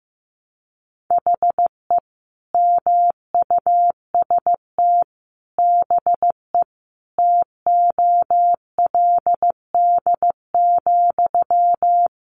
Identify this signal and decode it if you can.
Morse code, heard as a keyed tone, about 15 wpm, 700 Hz. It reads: HE MUST BE TOLD,
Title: Sentences from Top 1400 Words 15wpm